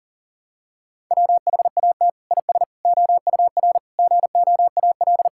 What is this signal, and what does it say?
Morse code, heard as a keyed tone, about 40 wpm, 700 Hz. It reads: WHAT IS OUR GOAL
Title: Sentences from Top 1000 Words - Review Part 5 - 40wpm